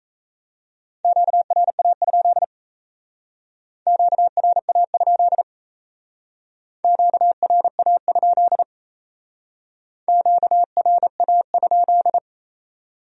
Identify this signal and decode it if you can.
Morse code, heard as a keyed tone, about 35 wpm, 700 Hz. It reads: QRA? QRA? QRA? QRA?
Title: Speed-Racing Q-Codes - 28wpm